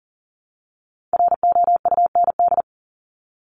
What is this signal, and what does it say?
Morse code, heard as a keyed tone, about 40 wpm, 700 Hz. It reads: ROUND
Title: Top 200 Words Repeated 40wpm